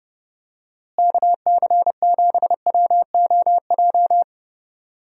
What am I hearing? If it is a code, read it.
Morse code, heard as a keyed tone, about 32 wpm, 700 Hz. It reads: KC7WOJ
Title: USA Call Signs - 6 Characters - Repeated 3x - 30wpm